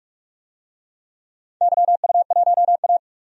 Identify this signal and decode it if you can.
Morse code, heard as a keyed tone, about 47 wpm, 700 Hz. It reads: YU1A
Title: DXCC-100 Call Signs - 4 Characters - 45wpm